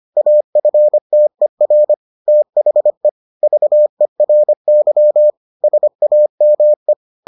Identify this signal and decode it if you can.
Morse code, heard as a keyed tone, about 25 wpm, 600 Hz. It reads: AFTER THE VERY SAME